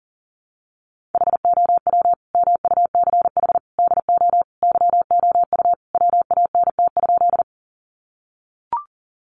Morse code, heard as a keyed tone, about 40 wpm, 700 Hz. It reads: HOW MUCH DO YOU WANT?
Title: Sentences from Top 200 Words 40wpm